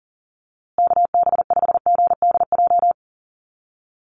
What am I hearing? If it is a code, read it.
Morse code, heard as a keyed tone, about 40 wpm, 700 Hz. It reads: KB5GDJ
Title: USA Call Signs Repeated 2x 40wpm